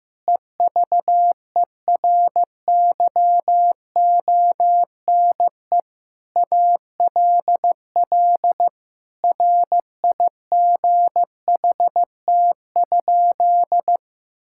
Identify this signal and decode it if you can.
Morse code, heard as a keyed tone, about 15 wpm, 700 Hz. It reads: EVERYONE ALL RIGHT?
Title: Sentences from Top 1000 Words - Review Part 5 - 15wpm